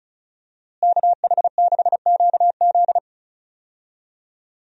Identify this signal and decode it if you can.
Morse code, heard as a keyed tone, about 36 wpm, 700 Hz. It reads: KH6QZ
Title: USA Call Signs - 5 Characters - Repeated 2x - 35wpm